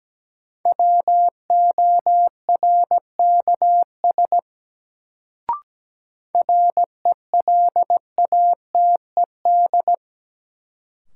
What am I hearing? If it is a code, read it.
Morse code, heard as a keyed tone, about 17 wpm, 700 Hz. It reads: WORKS RELATED
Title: Sentences from Top 1000 Words - Review Part 5 - New Word Drill - 17wpm